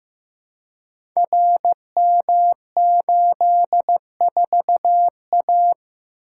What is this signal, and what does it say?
Morse code, heard as a keyed tone, about 15 wpm, 700 Hz. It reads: RM84A
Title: Code Groups - 5 Letter:Numbers - 15wpm